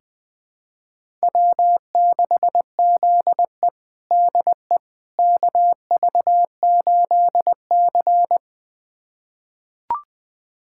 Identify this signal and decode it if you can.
Morse code, heard as a keyed tone, about 20 wpm, 700 Hz. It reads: W6ZE DE KV8C
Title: USA Callsign DE Callsign - 4 Characters - 20wpm